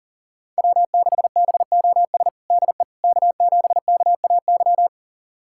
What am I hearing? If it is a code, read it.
Morse code, heard as a keyed tone, about 40 wpm, 690 Hz. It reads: W6BOS DE K7KAY
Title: USA Callsign DE Callsign - 5 Characters - 40wpm